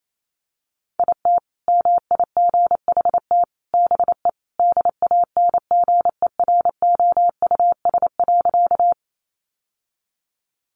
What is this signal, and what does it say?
Morse code, heard as a keyed tone, about 28 wpm, 700 Hz. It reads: IT MIGHT BE DANGEROUS.
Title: Sentences from Top 1200 Words 28wpm